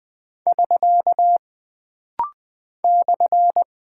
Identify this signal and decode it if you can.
Morse code, heard as a keyed tone, about 20 wpm, 700 Hz. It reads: <SK> /
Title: Single Punctuation-Prosign - Rapid-Fire - 20wpm